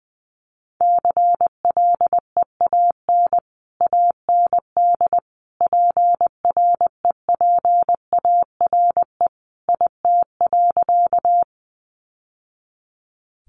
Morse code, heard as a keyed tone, about 20 wpm, 700 Hz. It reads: CLEAN AND PREPARE IT.
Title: Sentences from Top 1100 Words 20wpm